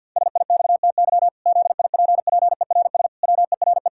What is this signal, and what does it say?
Morse code, heard as a keyed tone, about 50 wpm, 700 Hz. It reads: SIXTY ZIPPERS WERE
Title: Pangram Challenge v2 - 50wpm